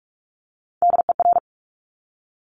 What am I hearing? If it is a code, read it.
Morse code, heard as a keyed tone, about 46 wpm, 690 Hz. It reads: DER
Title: Top Three Letter Combinations in Words 45wpm